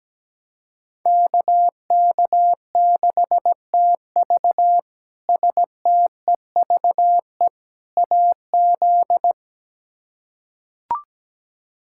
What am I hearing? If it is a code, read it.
Morse code, heard as a keyed tone, about 17 wpm, 700 Hz. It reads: KK6TV STEVE AZ
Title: K1USN SST QSO Practice - 17wpm